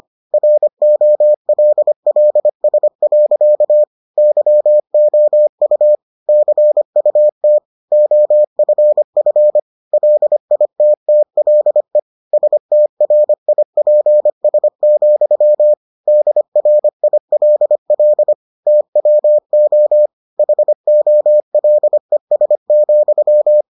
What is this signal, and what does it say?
Morse code, heard as a keyed tone, about 25 wpm, 600 Hz. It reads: ROLLS. YOU CUT OFF LITTLE STRIPS, DRILL TWO HOLES,